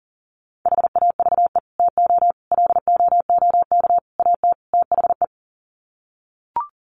Morse code, heard as a keyed tone, about 40 wpm, 700 Hz. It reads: HAVE TO LOOK AT THE
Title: Sets of 5 Words Encore 40wpm